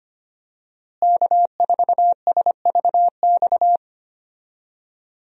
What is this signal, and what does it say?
Morse code, heard as a keyed tone, about 25 wpm, 700 Hz. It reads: K4SVX
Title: USA Call Signs - 5 Characters - Repeated 3x - 25wpm